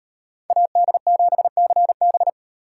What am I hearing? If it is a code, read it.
Morse code, heard as a keyed tone, about 36 wpm, 700 Hz. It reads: AD7CB